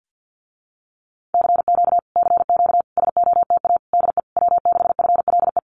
Morse code, heard as a keyed tone, about 50 wpm, 700 Hz. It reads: CQ CQ SOTA DE W6FLE
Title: Common QSO Phrases Encore 50wpm